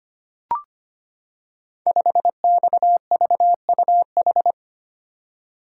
Morse code, heard as a keyed tone, about 25 wpm, 700 Hz. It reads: 5XVUH
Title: Code Groups - 5 Letter:Numbers - 25wpm